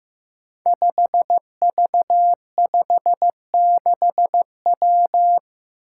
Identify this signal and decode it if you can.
Morse code, heard as a keyed tone, about 15 wpm, 700 Hz. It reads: 5V56W
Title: Five Challenging Letter-Number Combinations 15wpm